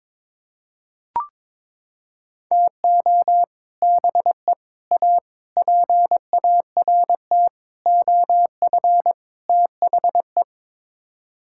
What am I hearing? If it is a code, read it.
Morse code, heard as a keyed tone, about 22 wpm, 700 Hz. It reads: TO BE A PART OF THE
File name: practice / 22wpm / Sets of 6 Words - 22wpm